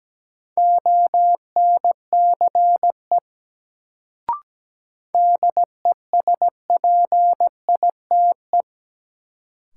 Morse code, heard as a keyed tone, about 17 wpm, 700 Hz. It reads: ONCE DESPITE
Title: Sets of 4 Words - New Word Drill - 17wpm